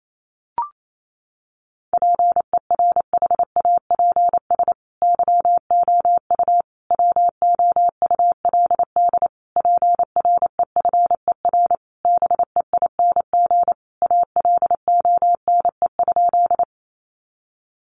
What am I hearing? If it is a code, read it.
Morse code, heard as a keyed tone, about 28 wpm, 700 Hz. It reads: PERHAPS YOU WOULD PREFER BEING ALONE?
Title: Sentences from Top 1100 Words 28wpm